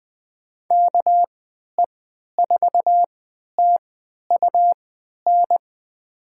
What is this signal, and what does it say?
Morse code, heard as a keyed tone, about 20 wpm, spaced as for 11 wpm, 700 Hz. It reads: KE4TUN
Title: USA Call Signs - 6 Characters - 3x Character Spacing - 20wpm